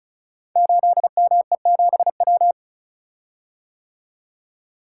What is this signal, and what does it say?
Morse code, heard as a keyed tone, about 36 wpm, 700 Hz. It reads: 8ME7W
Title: Code Groups - 5 Letter:Numbers - 35wpm